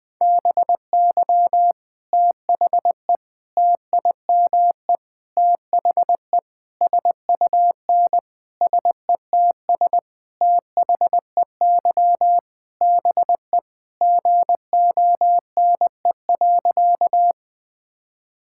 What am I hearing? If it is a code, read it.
Morse code, heard as a keyed tone, about 21 wpm, 700 Hz. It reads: BY THE TIME THE SUN SETS THEY BE GONE.
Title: Sentences with 4-Word Phrases - 20wpm